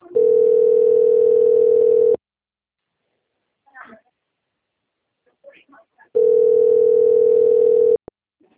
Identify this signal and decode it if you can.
Ringback tone (North America), 440 + 480 Hz, with speech.
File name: out- - - - - .4706